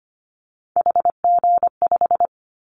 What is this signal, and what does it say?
Morse code, heard as a keyed tone, about 25 wpm, 700 Hz. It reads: HG5
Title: Callsign Prefixes for Top 25 DXCC - 25wpm